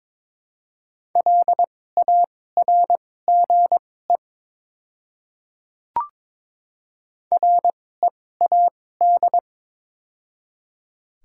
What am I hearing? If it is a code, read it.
Morse code, heard as a keyed tone, about 22 wpm, spaced as for 16 wpm, 700 Hz. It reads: LARGE READ
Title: 2x Character Spacing - Top 200 Words - 22wpm